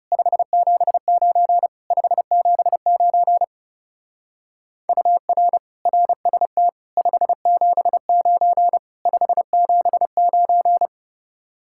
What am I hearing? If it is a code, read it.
Morse code, heard as a keyed tone, about 36 wpm, 700 Hz. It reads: 579 579 UR RST 579 579
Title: Speed-Racing Common QSO Phrases Encore - 30wpm